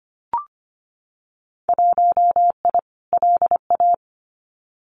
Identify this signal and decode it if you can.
Morse code, heard as a keyed tone, about 25 wpm, 700 Hz. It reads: 1I LA
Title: Winter Field Day Exchanges Encore 25wpm